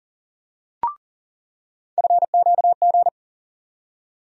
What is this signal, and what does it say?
Morse code, heard as a keyed tone, about 40 wpm, 700 Hz. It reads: FQG
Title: USA Call Sign Postfixes - 3 Letters 40wpm